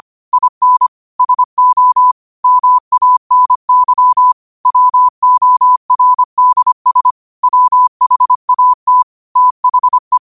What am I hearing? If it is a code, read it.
Morse code, heard as a keyed tone, about 25 wpm, 1000 Hz. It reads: IN SO MANY WORDS WHAT THE
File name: qotdfiles / 25wpm